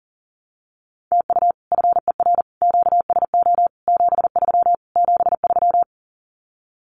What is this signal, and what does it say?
Morse code, heard as a keyed tone, about 40 wpm, 700 Hz. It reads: TU FER QSO 73 73
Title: Common QSO Phrases Encore 40wpm